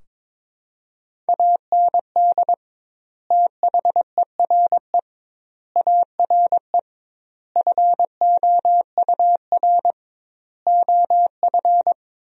Morse code, heard as a keyed tone, about 22 wpm, 700 Hz. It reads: AND THERE ARE FOUR OF
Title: 2x Word-Spacing - Sentences from Top 300 Words - 22wpm